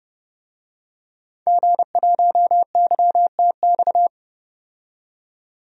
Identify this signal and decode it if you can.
Morse code, heard as a keyed tone, about 31 wpm, 700 Hz. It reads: G1YTX
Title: DXCC-100 Call Signs - 5 Characters - 30wpm